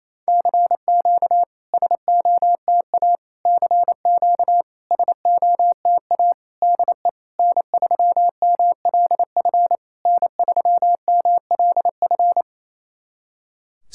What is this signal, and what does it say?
Morse code, heard as a keyed tone, about 28 wpm, 700 Hz. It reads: CQ SOTA CQ SOTA DE N3MLF N3MLF
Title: SOTA QSO Practice 28wpm